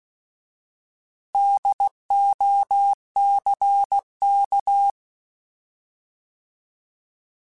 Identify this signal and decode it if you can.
Morse code, heard as a keyed tone, about 16 wpm, 800 Hz. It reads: DOCK
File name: Alphabetical / Alphabetical mp3s / Ds @15